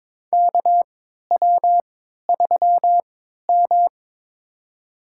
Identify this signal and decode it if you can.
Morse code, heard as a keyed tone, about 22 wpm, spaced as for 13 wpm, 700 Hz. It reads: KW3M
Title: USA Call Signs - 4 Characters - 3x Character Spacing - 22wpm